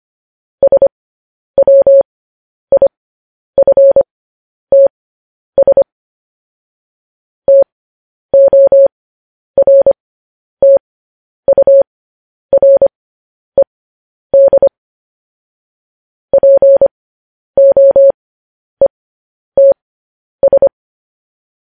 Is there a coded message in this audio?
Yes: Morse code from 0.6 to 20.7 s, heard as a keyed tone, about 25 wpm, spaced as for 10 wpm, 550 Hz. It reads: SWIFTS TORTURED POETS